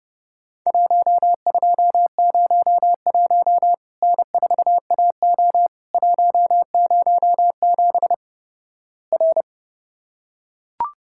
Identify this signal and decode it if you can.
Morse code, heard as a keyed tone, about 32 wpm, 700 Hz. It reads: 1201 N4AO 107 R
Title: Worked All Europe DX Contest - QTC Exchange - 30wpm